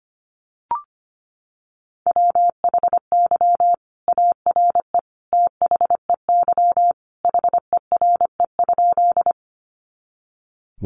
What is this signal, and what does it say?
Morse code, heard as a keyed tone, about 25 wpm, 700 Hz. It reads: WHY ARE THEY HERE?
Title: Sentences from Top 200 Words 25wpm